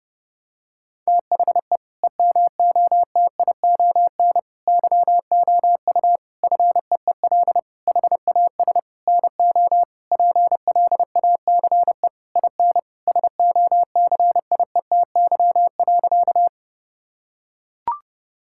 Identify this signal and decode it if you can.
Morse code, heard as a keyed tone, about 30 wpm, 700 Hz. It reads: THE EMOTION YOU FEEL HAS NO PLACE IN SOCIETY.
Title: Sentences from Top 1600 Words 30wpm